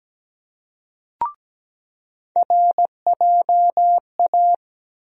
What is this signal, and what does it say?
Morse code, heard as a keyed tone, about 17 wpm, 700 Hz. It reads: RJA
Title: USA Call Sign Postfixes - 3 Letters 17wpm